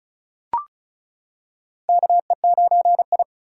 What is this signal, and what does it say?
Morse code, heard as a keyed tone, about 36 wpm, 700 Hz. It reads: KE9I
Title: K1USN SST QSO Practice - 35wpm